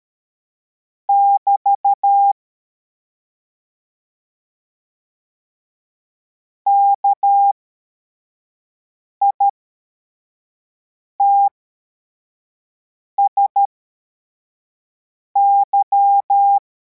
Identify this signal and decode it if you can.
Morse code, heard as a keyed tone, about 13 wpm, spaced as for 4 wpm, 800 Hz. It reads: <BT> KITSY
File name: formation / cw / Text55